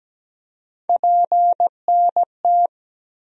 Morse code, heard as a keyed tone, about 17 wpm, 700 Hz. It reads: PNT
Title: USA Call Sign Postfixes - 3 Letters 17wpm